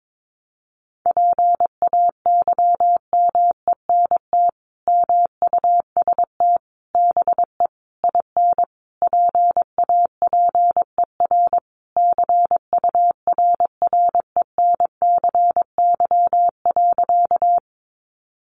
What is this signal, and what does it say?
Morse code, heard as a keyed tone, about 22 wpm, 700 Hz. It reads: PAYMENT MUST BE IN PAPER CURRENCY.
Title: Sentences from Top 1500 Words 22wpm